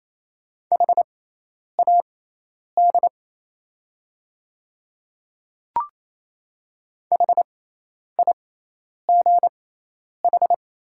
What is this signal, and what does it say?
Morse code, heard as a keyed tone, about 28 wpm, spaced as for 10 wpm, 700 Hz. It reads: HAD HIGH
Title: 6x Character Spacing - Top 300 Words - 28wpm